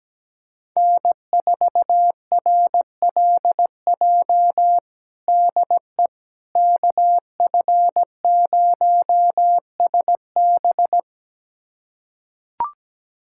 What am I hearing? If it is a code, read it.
Morse code, heard as a keyed tone, about 17 wpm, 700 Hz. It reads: N4RLJ DE KF0SB
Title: USA Callsign DE Callsign - 5 Characters - 17wpm